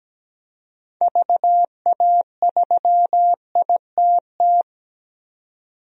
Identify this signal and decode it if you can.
Morse code, heard as a keyed tone, about 17 wpm, 700 Hz. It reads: VA3ITT
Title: DXCC-100 Call Signs - 6 Characters - 17wpm